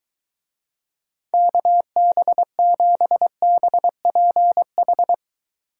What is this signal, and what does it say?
Morse code, heard as a keyed tone, about 23 wpm, 700 Hz. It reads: KB7BPH